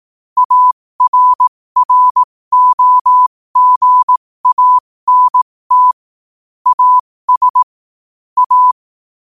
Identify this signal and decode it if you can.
Morse code, heard as a keyed tone, about 18 wpm, spaced as for 15 wpm, 1000 Hz. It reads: ARROGANT AS A
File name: qotdfiles / 15wpm